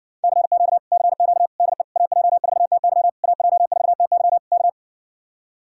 Morse code, heard as a keyed tone, about 60 wpm, 700 Hz. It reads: CQ CQ DE AJ4TY AJ4TY K